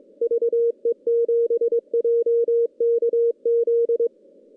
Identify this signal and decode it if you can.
Morse code, heard as a keyed tone, about 22 wpm, 470 Hz. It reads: VE7JKZ
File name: contests / CW-SS-k5tr- -0000